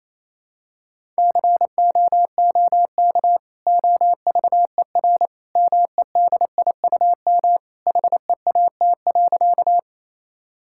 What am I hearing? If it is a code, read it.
Morse code, heard as a keyed tone, about 27 wpm, 700 Hz. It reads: COOK OVER MEDIUM HEAT.
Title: Sentences from Top 1000 Words - Review Part 6 - 28wpm